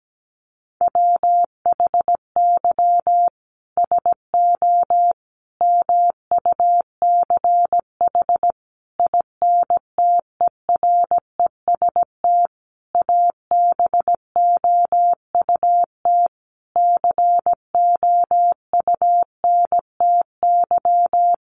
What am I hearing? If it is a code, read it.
Morse code, heard as a keyed tone, about 17 wpm, 700 Hz. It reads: WHY SO MUCH INTEREST ABOUT COUNTY
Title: Sentences from Top 1200 Words 17wpm